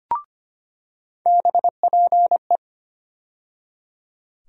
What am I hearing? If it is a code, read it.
Morse code, heard as a keyed tone, about 25 wpm, 700 Hz. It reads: BPE
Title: USA Call Sign Postfixes - 3 Letters 25wpm